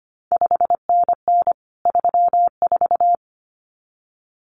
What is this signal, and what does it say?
Morse code, heard as a keyed tone, about 25 wpm, 700 Hz. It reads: 5NN 34
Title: CQ WW DX Contest Exchange Encore - 25wpm